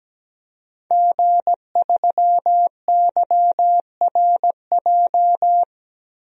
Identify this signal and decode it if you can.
Morse code, heard as a keyed tone, about 17 wpm, 700 Hz. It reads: G3YRJ
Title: DXCC-100 Call Signs - 5 Characters - 17wpm